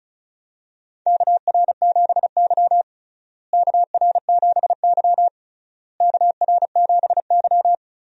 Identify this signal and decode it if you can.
Morse code, heard as a keyed tone, about 36 wpm, 700 Hz. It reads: KR7Y KR7Y KR7Y
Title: USA Call Signs - 4 Characters - Repeated 3x - 35wpm